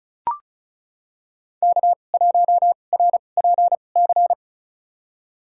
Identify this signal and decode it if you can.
Morse code, heard as a keyed tone, about 36 wpm, spaced as for 26 wpm, 700 Hz. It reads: K1RPC
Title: USA Call Signs - 5 Characters - 2x Character Spacing - 35wpm